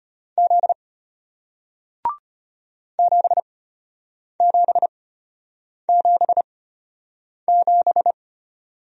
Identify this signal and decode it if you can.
Morse code, heard as a keyed tone, about 33 wpm, spaced as for 8 wpm, 700 Hz. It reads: Z 7777